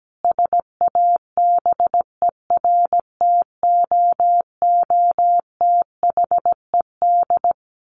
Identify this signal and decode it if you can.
Morse code, heard as a keyed tone, about 17 wpm, 700 Hz. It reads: SABERTOOTHED
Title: A Sound Of Thunder 17wpm